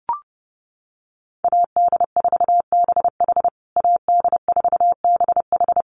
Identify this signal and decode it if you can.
Morse code, heard as a keyed tone, about 30 wpm, 700 Hz. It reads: AD4BH AD4BH
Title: USA Call Signs Repeated 2x 30wpm